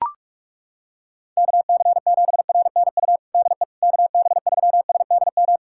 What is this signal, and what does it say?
Morse code, heard as a keyed tone, about 41 wpm, 700 Hz. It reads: KC7RNU DE KB3SDM
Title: USA Callsign DE Callsign - 6 Characters - 45wpm